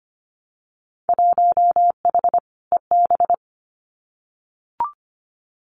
Morse code, heard as a keyed tone, about 25 wpm, 700 Hz. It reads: 1H EB
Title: Winter Field Day Exchanges 25wpm